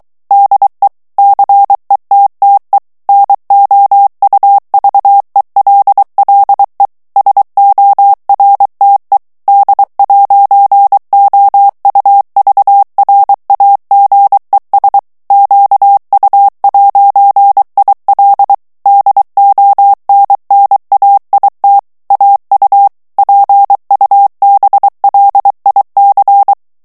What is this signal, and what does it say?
Morse code, heard as a keyed tone, about 23 wpm, 800 Hz. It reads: DE CETTE NOUVELLE SORTE D'OUVRAGES QU'IL DONNAIT AU PUBLIC
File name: CW / FICHIERS-MP3 / FRANCAIS TEXTE CLAIR 2 VITESSE 22 MOTS MINUTES